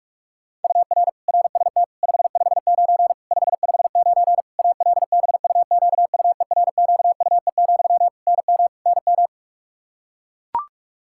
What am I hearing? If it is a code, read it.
Morse code, heard as a keyed tone, about 42 wpm, 700 Hz. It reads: UR RST 559 559 ALBUQUERQUE, NM NM
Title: Practice QSO - Exchange 1 - Beginning - 45wpm